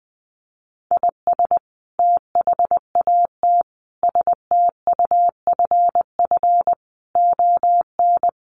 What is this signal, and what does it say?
Morse code, heard as a keyed tone, about 20 wpm, 700 Hz. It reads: IS THAT STUFF ON
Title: Sentences from Top 1400 Words 20wpm